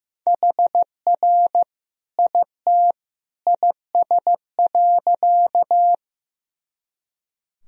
Morse code, heard as a keyed tone, about 16 wpm, 700 Hz. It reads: HR IT IS.
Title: Sentences from Top 200 Words and Abbreviations 15wpm